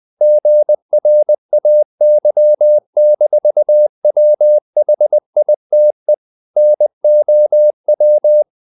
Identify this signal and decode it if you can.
Morse code, heard as a keyed tone, about 20 wpm, 600 Hz. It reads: GRAY-WHITE NOW
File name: the old man and the sea part 2 20wpm